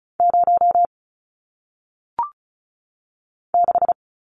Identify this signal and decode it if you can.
Morse code, heard as a keyed tone, about 36 wpm, 700 Hz. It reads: 0 6
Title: Numbers - 1-Digit 35wpm